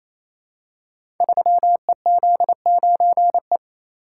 Morse code, heard as a keyed tone, about 28 wpm, 700 Hz. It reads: 3EZ9E
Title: Code Groups - 5 Letter:Numbers - 28wpm